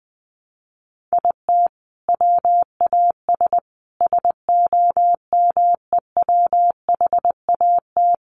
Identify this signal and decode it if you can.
Morse code, heard as a keyed tone, about 20 wpm, 700 Hz. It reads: IT WAS SOMEWHAT
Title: Sentences from Top 1300 Words 20wpm